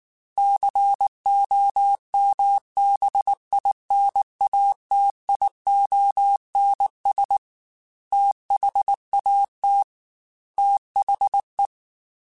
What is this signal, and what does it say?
Morse code, heard as a keyed tone, about 19 wpm, 800 Hz. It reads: COMBINATIONS THAT THE
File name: Technical Subjects / Technical Subjects-mp3s / AND Gate @18